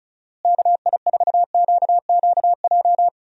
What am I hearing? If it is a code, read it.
Morse code, heard as a keyed tone, about 34 wpm, 700 Hz. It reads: KI4QQJ
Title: USA Call Signs - 6 Characters - Repeated 3x - 35wpm